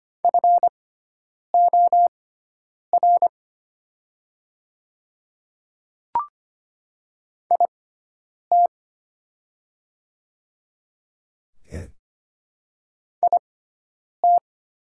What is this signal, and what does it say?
Morse code, heard as a keyed tone, about 25 wpm, spaced as for 9 wpm, 700 Hz. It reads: FOR IT IT
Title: 6x Character Spacing - Top 100 Words - 25wpm